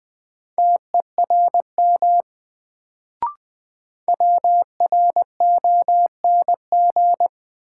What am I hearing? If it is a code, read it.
Morse code, heard as a keyed tone, about 20 wpm, 700 Hz. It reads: TERM WRONG
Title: Sentences from Top 900 Words - New Word Drill - 20wpm